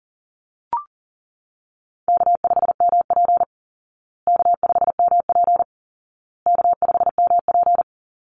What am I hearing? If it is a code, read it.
Morse code, heard as a keyed tone, about 40 wpm, 700 Hz. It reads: K5MP K5MP K5MP
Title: Easy Call Signs Repeated 3x - 40wpm